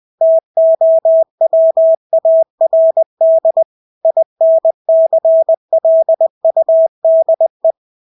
Morse code, heard as a keyed tone, about 20 wpm, 650 Hz. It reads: TOWARD INCLUDE
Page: Engelske ord 20 wpm